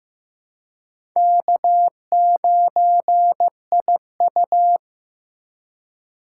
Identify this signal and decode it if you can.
Morse code, heard as a keyed tone, about 15 wpm, 700 Hz. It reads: K9IU
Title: USA Call Signs - 4 Characters - 3x Character Spacing - 15wpm